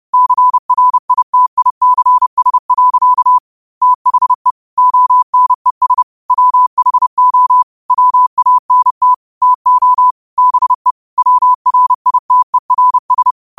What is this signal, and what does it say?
Morse code, heard as a keyed tone, about 30 wpm, 1000 Hz. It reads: CRITICS. THE ONES WHO WANT TO BE WRITERS
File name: qotdfiles / 30wpm